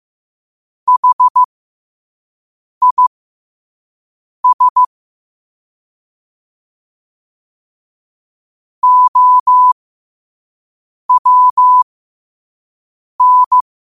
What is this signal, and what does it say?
Morse code, heard as a keyed tone, about 15 wpm, spaced as for 5 wpm, 1000 Hz. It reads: HIS OWN
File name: qotdfiles / 5wpm